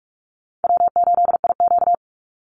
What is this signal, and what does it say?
Morse code, heard as a keyed tone, about 44 wpm, 700 Hz. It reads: W8IQ
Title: Easy Call Signs - 45wpm